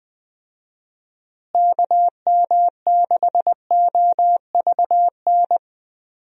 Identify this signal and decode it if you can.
Morse code, heard as a keyed tone, about 20 wpm, 700 Hz. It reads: KM6OVN
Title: USA Call Signs - 6 Characters - Repeated 2x - 20wpm